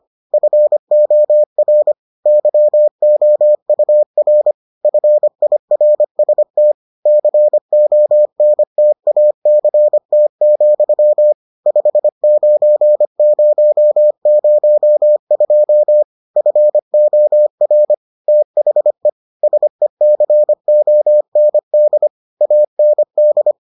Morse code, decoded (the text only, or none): FOR YOUR FIRST CONTACT, 59002 FOR THE SECOND AND